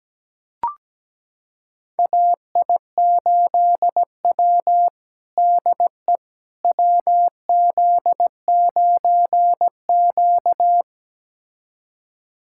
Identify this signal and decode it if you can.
Morse code, heard as a keyed tone, about 17 wpm, 700 Hz. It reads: AI8W DE WZ9Q
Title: USA Callsign DE Callsign - 4 Characters - 17wpm